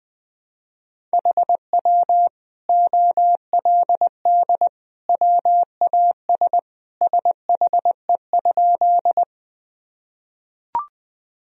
Morse code, heard as a keyed tone, about 21 wpm, 700 Hz. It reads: HW OLD WAS SHE?
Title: Sentences from Top 200 Words and Abbreviations 20wpm